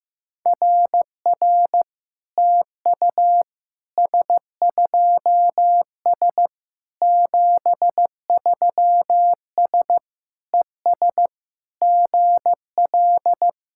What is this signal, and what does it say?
Morse code, heard as a keyed tone, about 15 wpm, 700 Hz. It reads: RR TU S2S 73S ES GL
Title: SOTA QSO Practice 15wpm